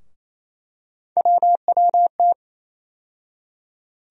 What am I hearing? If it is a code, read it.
Morse code, heard as a keyed tone, about 28 wpm, 700 Hz. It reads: WWT
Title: USA Call Sign Postfixes - 3 Letters 28wpm